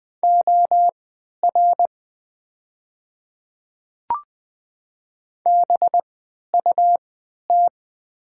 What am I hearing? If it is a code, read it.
Morse code, heard as a keyed tone, about 20 wpm, spaced as for 11 wpm, 700 Hz. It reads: OR BUT
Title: 3x Character Spacing - Top 100 Words - 20wpm